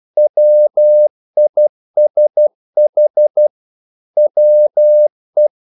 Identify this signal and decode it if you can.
Morse code, heard as a keyed tone, about 12 wpm, 600 Hz. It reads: WISH WE
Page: English quotations in CW (telegraphy - morse)